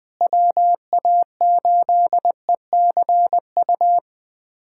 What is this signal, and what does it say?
Morse code, heard as a keyed tone, about 20 wpm, 700 Hz. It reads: WA8ECU
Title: USA Callsign DE Callsign - 6 Characters - 20wpm